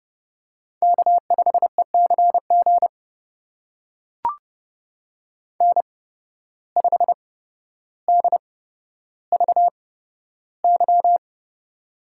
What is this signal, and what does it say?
Morse code, heard as a keyed tone, about 30 wpm, spaced as for 13 wpm, 700 Hz. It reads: K5ECG N5DVY
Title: USA Call Signs - 5 Characters - 8x Character Spacing - 30wpm